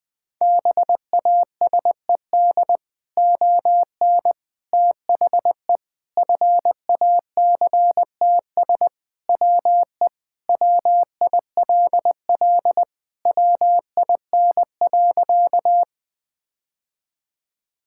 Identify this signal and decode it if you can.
Morse code, heard as a keyed tone, about 20 wpm, 700 Hz. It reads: BASED ON THE FACTS WE WILL WIN.
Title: Sentences with 3-Word Phrases - 20wpm